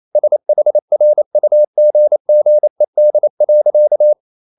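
Morse code, heard as a keyed tone, about 28 wpm, 600 Hz. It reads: SHRUGGED.
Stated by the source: Morse code practice at 28 wpm, text of Ken Follett's novel Never in English, from CW med Gnister.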